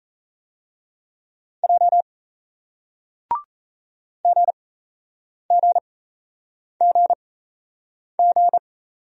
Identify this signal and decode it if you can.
Morse code, heard as a keyed tone, about 38 wpm, spaced as for 8 wpm, 700 Hz. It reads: J GGGG